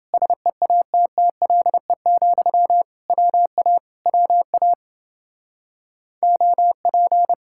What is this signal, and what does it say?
Morse code, heard as a keyed tone, about 28 wpm, 700 Hz. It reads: SEATTLE, WA WA OP